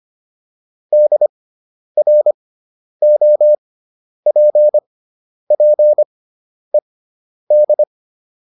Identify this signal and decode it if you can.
Morse code, heard as a keyed tone, about 25 wpm, spaced as for 10 wpm, 600 Hz. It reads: DROPPED